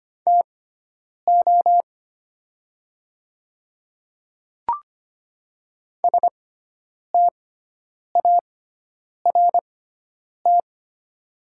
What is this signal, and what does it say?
Morse code, heard as a keyed tone, about 25 wpm, spaced as for 9 wpm, 700 Hz. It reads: TO START
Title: 6x Character Spacing - Top 300 Words - 25wpm